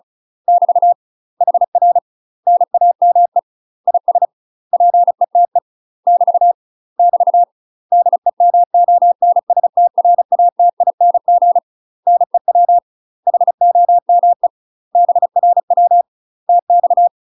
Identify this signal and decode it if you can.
Morse code, heard as a keyed tone, about 35 wpm, 700 Hz. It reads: <BT> HR NAME IS PETE <BT> <BT> DEMONSTRATING NEW HOME BRW TX
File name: UK MASTER QSOs TRAINING / 35WPM / UK MASTER QSOs 0038